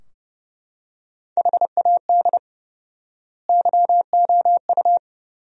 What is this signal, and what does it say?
Morse code, heard as a keyed tone, about 30 wpm, 700 Hz. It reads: HAD YOU
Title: 4x Word-Spacing - Sentences from Top 400 Words - 30wpm